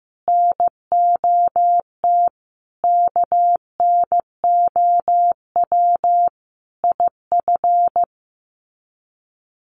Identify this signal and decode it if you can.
Morse code, heard as a keyed tone, about 15 wpm, 700 Hz. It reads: NOT KNOW IF
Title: Sets of 5 Words Encore 15wpm